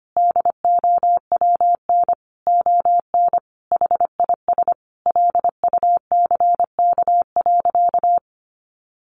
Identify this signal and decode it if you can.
Morse code, heard as a keyed tone, about 25 wpm, 700 Hz. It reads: DOWN ON HIS LUCK.
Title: Sentences from Top 1500 Words 25wpm